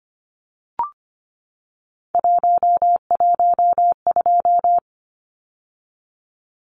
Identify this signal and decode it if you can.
Morse code, heard as a keyed tone, about 25 wpm, 700 Hz. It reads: 112
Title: Numbers - 3-Digit 25wpm